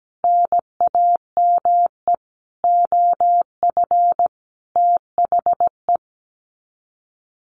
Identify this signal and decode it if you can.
Morse code, heard as a keyed tone, about 17 wpm, 690 Hz. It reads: NAME OF THE
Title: Sets of 5 Words Encore 17wpm